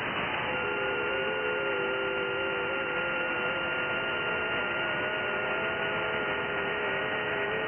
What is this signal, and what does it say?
FT8.
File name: IC-7100